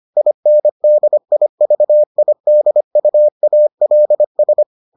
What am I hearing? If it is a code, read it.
Morse code, heard as a keyed tone, about 25 wpm, 600 Hz. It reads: INDIVIDUALS